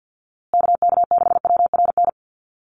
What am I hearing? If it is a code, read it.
Morse code, heard as a keyed tone, about 49 wpm, 700 Hz. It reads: KK6WRN
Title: USA Call Signs Repeated 2x 50wpm